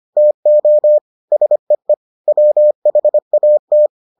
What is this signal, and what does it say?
Morse code, heard as a keyed tone, about 25 wpm, 600 Hz. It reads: TO SEE WHAT